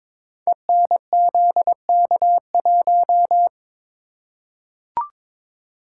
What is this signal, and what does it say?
Morse code, heard as a keyed tone, about 22 wpm, 700 Hz. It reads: ENZK1
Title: Code Groups - 5 Letter:Numbers - 22wpm